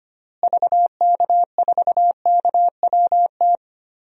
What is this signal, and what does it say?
Morse code, heard as a keyed tone, about 25 wpm, 700 Hz. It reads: VK4KWT
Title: DXCC-100 Call Signs - 6 Characters - Repeated 2x - 25wpm